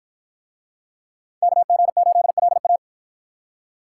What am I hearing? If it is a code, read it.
Morse code, heard as a keyed tone, about 53 wpm, 700 Hz. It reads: KG8LA